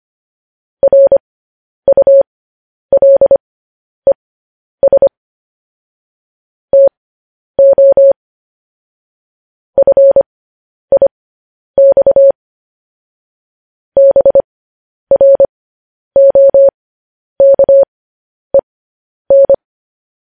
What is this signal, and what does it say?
Morse code, heard as a keyed tone, about 25 wpm, spaced as for 10 wpm, 550 Hz. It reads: RULES TO FIX BROKEN
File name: podcast / News headlines-1x-25x10